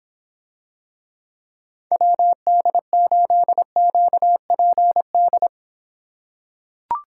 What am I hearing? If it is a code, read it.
Morse code, heard as a keyed tone, about 26 wpm, 700 Hz. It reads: WD8QPD